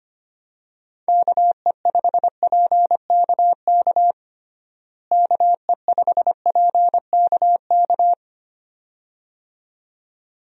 Morse code, heard as a keyed tone, about 25 wpm, 700 Hz. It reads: KE5PKK KE5PKK
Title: USA Call Signs - 6 Characters - Repeated 2x - 25wpm